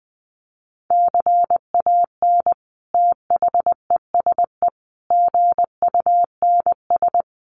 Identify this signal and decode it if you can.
Morse code, heard as a keyed tone, about 20 wpm, 700 Hz. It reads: CAN THESE GUNS
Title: A Sound Of Thunder 20wpm